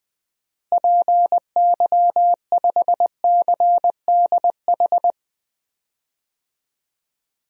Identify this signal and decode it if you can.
Morse code, heard as a keyed tone, about 20 wpm, 700 Hz. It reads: PY5CDH
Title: DXCC-100 Call Signs - 6 Characters - Repeated 3x - 20wpm